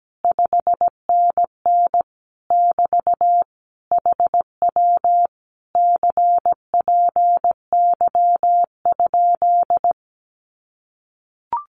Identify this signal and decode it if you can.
Morse code, heard as a keyed tone, about 17 wpm, 700 Hz. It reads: 5NN <BT> HW CPY?
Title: Common QSO Phrases Encore 17wpm